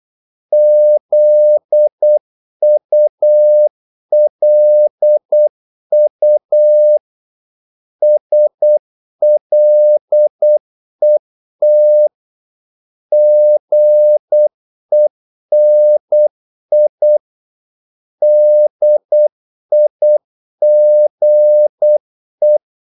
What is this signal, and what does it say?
Morse code, heard as a keyed tone, about 8 wpm, 600 Hz. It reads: ZULU SLET GENI DIGE
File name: Korte ord DK 8wpm